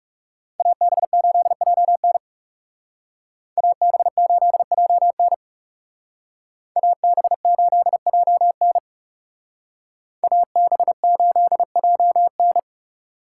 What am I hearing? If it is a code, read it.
Morse code, heard as a keyed tone, about 38 wpm, 700 Hz. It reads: AB8JN AB8JN AB8JN AB8JN